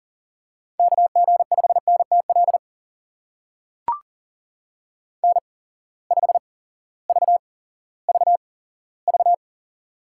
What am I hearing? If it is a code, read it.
Morse code, heard as a keyed tone, about 40 wpm, spaced as for 18 wpm, 700 Hz. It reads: KG5NTL N5VVV
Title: USA Call Signs - 8x Character Spacing - 40wpm